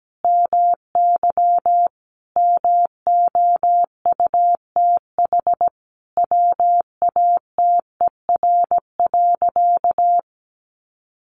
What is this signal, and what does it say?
Morse code, heard as a keyed tone, about 17 wpm, 700 Hz. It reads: MY MOUTH WATER.
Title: Sentences from Top 1500 Words 17wpm